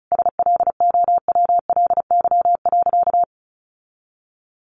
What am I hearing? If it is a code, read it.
Morse code, heard as a keyed tone, about 34 wpm, 700 Hz. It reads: SLOWLY.
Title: Sentences from Top 1100 Words 35wpm